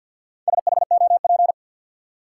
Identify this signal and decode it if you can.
Morse code, heard as a keyed tone, about 53 wpm, 700 Hz. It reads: SHOP
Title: Four Letter Words - 50wpm